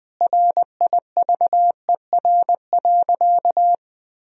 Morse code, heard as a keyed tone, about 20 wpm, 700 Hz. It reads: RIVER.
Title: Sentences from Top 1000 Words - Review Part 4 - 20wpm